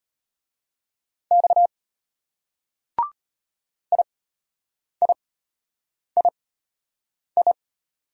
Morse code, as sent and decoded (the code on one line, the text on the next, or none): -..- / .. .. .. ..
X IIII